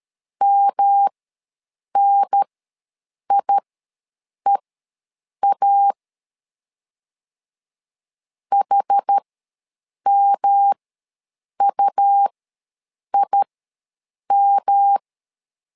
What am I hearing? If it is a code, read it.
Morse code, heard as a keyed tone, about 12 wpm, spaced as for 7 wpm, 800 Hz. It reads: MNIEA HMUIM
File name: LEZIONI / 3 MWRO / 3 M ws 12wpm 1745